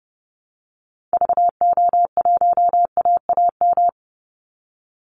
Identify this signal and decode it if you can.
Morse code, heard as a keyed tone, about 30 wpm, 700 Hz. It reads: VO1AAM
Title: Call Signs Repeated 2x - 30wpm